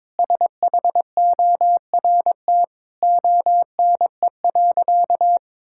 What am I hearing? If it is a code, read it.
Morse code, heard as a keyed tone, about 22 wpm, 700 Hz. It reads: SHORT ONE.
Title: Sentences from Top 1600 Words 22wpm